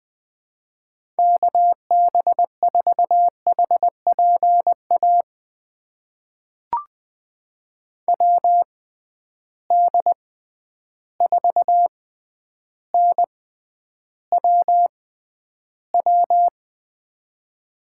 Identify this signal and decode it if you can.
Morse code, heard as a keyed tone, about 20 wpm, spaced as for 10 wpm, 700 Hz. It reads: KB4HPA WD4NWW
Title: USA Call Signs - 6 Characters - 6x Character Spacing - 20wpm